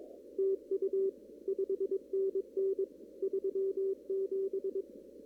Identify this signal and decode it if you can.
Morse code, heard as a keyed tone, about 22 wpm, 390 Hz. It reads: TU 5NN 37
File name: CW / kt5j wpx cw-op-k5pi